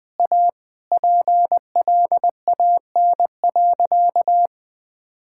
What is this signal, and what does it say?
Morse code, heard as a keyed tone, about 21 wpm, 700 Hz. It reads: A PLAN.
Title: Sentences with 3-Word Phrases - 20wpm